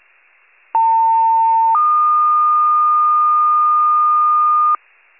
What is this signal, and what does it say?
Two-tone page, heard as a one-second tone, then a three-second tone, about 900 then 1220 Hz. It is heard from 0.7 to 4.8 s.
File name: wcalert